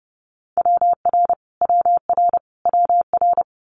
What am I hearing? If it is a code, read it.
Morse code, heard as a keyed tone, about 30 wpm, 700 Hz. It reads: WR WR WR
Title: Multi-letter Phonemes Repeated 3x 30wpm